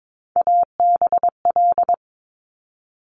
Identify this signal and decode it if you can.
Morse code, heard as a keyed tone, about 22 wpm, 700 Hz. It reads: ABL
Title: Top Three Letter Combinations in Words 22wpm